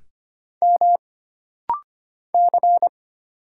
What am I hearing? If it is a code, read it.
Morse code, heard as a keyed tone, about 25 wpm, 700 Hz. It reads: M C
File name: Single Character (Letter:Number:Punctuation:Prosign) - Rapid-Fire - 25wpm